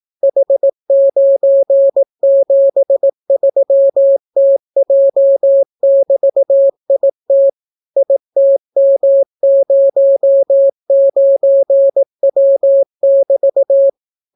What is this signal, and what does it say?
Morse code, heard as a keyed tone, about 18 wpm, 550 Hz. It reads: H973TJ<BT>IT ITM09W<BT>